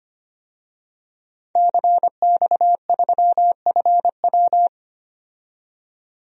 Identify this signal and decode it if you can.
Morse code, heard as a keyed tone, about 25 wpm, 700 Hz. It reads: CX3FW
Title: DXCC-100 Call Signs - 5 Characters - 25wpm